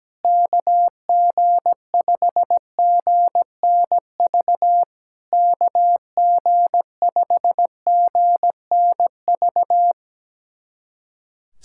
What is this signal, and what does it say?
Morse code, heard as a keyed tone, about 17 wpm, 700 Hz. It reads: KG5GNV KG5GNV
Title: SOTA QSO Practice 17wpm